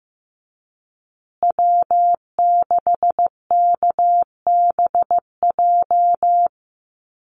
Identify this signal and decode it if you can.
Morse code, heard as a keyed tone, about 15 wpm, 700 Hz. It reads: W6KBJ
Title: USA Call Signs Repeated 3x 15wpm